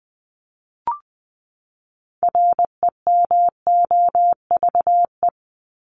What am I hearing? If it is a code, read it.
Morse code, heard as a keyed tone, about 20 wpm, 700 Hz. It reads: REMOVE
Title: ED Suffix - Top 100 Words with Base Words 20wpm